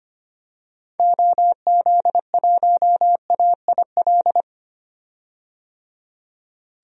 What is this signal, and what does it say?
Morse code, heard as a keyed tone, about 25 wpm, 700 Hz. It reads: OZ1AIL
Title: DXCC-100 Call Signs - 6 Characters - 25wpm